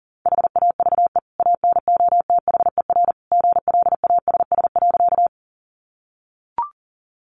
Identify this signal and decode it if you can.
Morse code, heard as a keyed tone, about 40 wpm, 700 Hz. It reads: HAVE ANOTHER GLASS.
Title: Sentences from Top 1300 Words 40wpm